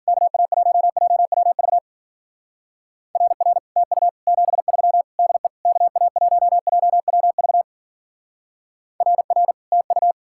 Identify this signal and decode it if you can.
Morse code, heard as a keyed tone, about 49 wpm, 700 Hz. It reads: KA1JWV RR TU 73 DE KA1JWV RR TU
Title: Speed-Racing Common QSO Phrases - 40wpm